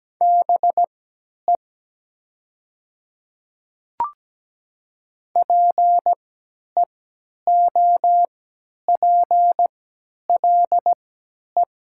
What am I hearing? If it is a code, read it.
Morse code, heard as a keyed tone, about 17 wpm, spaced as for 10 wpm, 700 Hz. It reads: BE PEOPLE
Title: 3x Character Spacing - Top 100 Words - 17wpm